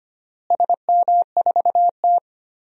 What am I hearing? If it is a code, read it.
Morse code, heard as a keyed tone, about 25 wpm, 700 Hz. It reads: SM4T
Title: DXCC-100 Call Signs - 4 Characters - Repeated 3x - 25wpm